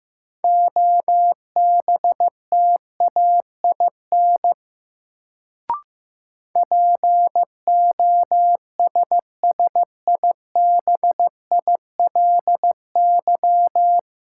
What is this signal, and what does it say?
Morse code, heard as a keyed tone, about 15 wpm, 700 Hz. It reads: OBTAIN POSSIBILY
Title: Sentences from Top 1300 Words - New Word Drill - 15wpm